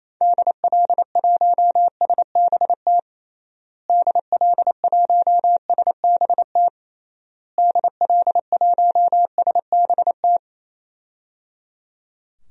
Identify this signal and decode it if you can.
Morse code, heard as a keyed tone, about 27 wpm, 700 Hz. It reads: DL1SBT DL1SBT DL1SBT
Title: DXCC-100 Call Signs - 6 Characters - Repeated 3x - 28wpm